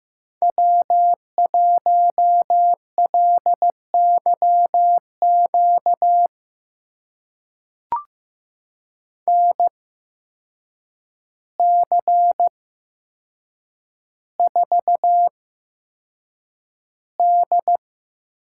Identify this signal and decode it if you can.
Morse code, heard as a keyed tone, about 15 wpm, spaced as for 7 wpm, 700 Hz. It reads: W1LYQ NC4D
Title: USA Call Signs - 8x Character Spacing - 15wpm